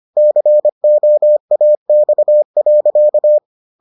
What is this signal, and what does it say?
Morse code, heard as a keyed tone, about 25 wpm, 600 Hz. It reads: COAX.